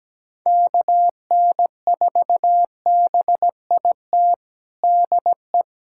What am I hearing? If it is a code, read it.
Morse code, heard as a keyed tone, about 17 wpm, 700 Hz. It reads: KN4BIT DE
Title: USA Callsign DE Callsign - 6 Characters - 17wpm